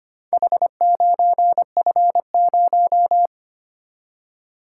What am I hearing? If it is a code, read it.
Morse code, heard as a keyed tone, about 25 wpm, 700 Hz. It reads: H9F0
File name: practice / 25wpm / Four Letter-Number Combinations - 25wpm